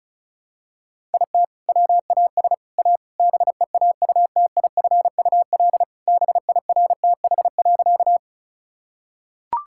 Morse code, heard as a keyed tone, about 34 wpm, 700 Hz. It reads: IT WAS A BEAUTIFUL BIRTH.
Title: Sentences from Top 1000 Words - Review Part 5 - 35wpm